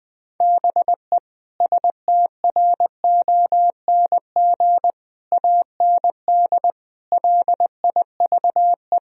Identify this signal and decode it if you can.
Morse code, heard as a keyed tone, about 21 wpm, 700 Hz. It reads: BE STRONG AND LIVE
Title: Sentences from Top 1000 Words - Review Part 3 - 20wpm